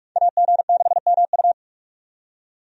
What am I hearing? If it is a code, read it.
Morse code, heard as a keyed tone, about 41 wpm, 700 Hz. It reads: AG6MU
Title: USA Call Signs - 5 Characters - 45wpm